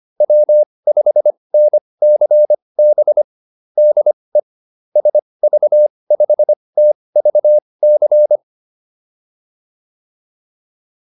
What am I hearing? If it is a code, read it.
Morse code, heard as a keyed tone, about 25 wpm, spaced as for 20 wpm, 600 Hz. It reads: W5NCB DE SV5TVC
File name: QSO408 20